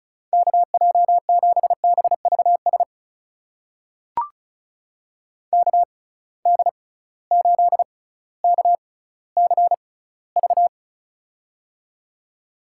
Morse code, heard as a keyed tone, about 34 wpm, spaced as for 18 wpm, 700 Hz. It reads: KJ7BVS KD8KCV
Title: USA Call Signs - 6 Characters - 6x Character Spacing - 35wpm